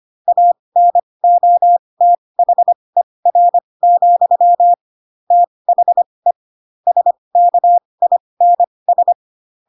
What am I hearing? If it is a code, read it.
Morse code, heard as a keyed tone, about 25 wpm, spaced as for 20 wpm, 700 Hz. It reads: ANOTHER, THE SKINS
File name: Pinocchio / Pinocchio ch7 15 25 ik1hge